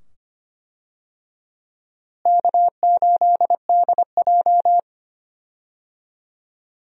Morse code, heard as a keyed tone, about 25 wpm, 700 Hz. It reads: K8DJ
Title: USA Call Signs - 4 Characters - Repeated 2x - 25wpm